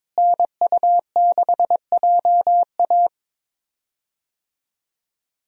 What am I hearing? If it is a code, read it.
Morse code, heard as a keyed tone, about 22 wpm, 700 Hz. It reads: NU6JA
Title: USA Call Signs - 5 Characters - Repeated 2x - 22wpm